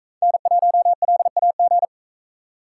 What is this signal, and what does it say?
Morse code, heard as a keyed tone, about 39 wpm, 700 Hz. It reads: N1LAG